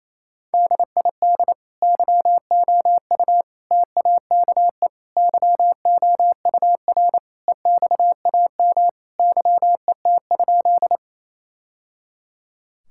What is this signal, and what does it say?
Morse code, heard as a keyed tone, about 28 wpm, 700 Hz. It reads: DID YOU TAKE YOUR EXAM YET?
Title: Sentences from Top 1000 Words - Review Part 5 - 28wpm